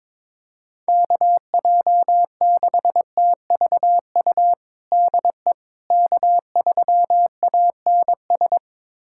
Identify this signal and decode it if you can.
Morse code, heard as a keyed tone, about 22 wpm, 700 Hz. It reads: KJ6TVU DE K3ANS
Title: USA Callsign DE Callsign 22wpm